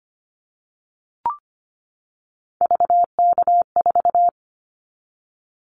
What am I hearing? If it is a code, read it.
Morse code, heard as a keyed tone, about 25 wpm, 700 Hz. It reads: VK4
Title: Callsign Prefixes for Top 50 DXCC - 25wpm